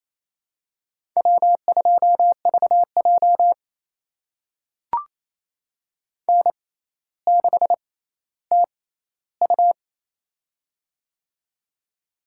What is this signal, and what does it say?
Morse code, heard as a keyed tone, about 28 wpm, spaced as for 14 wpm, 700 Hz. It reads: W2VJ N6TU
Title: USA Call Signs - 4 Characters - 6x Character Spacing - 28wpm